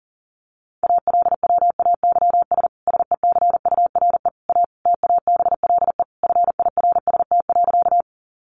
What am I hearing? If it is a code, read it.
Morse code, heard as a keyed tone, about 40 wpm, 700 Hz. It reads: ALWAYS SECURE A TABLE FIRST.
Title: Sentences from Top 1200 Words 40wpm